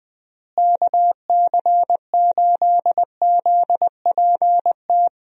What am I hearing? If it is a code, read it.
Morse code, heard as a keyed tone, about 20 wpm, 700 Hz. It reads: KC8ZPT
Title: USA Call Signs - 6 Characters - Repeated 3x - 20wpm